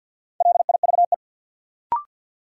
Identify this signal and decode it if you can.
Morse code, heard as a keyed tone, about 52 wpm, 700 Hz. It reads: LIVE
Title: Sentences from Top 200 Words - New Word Drill - 50wpm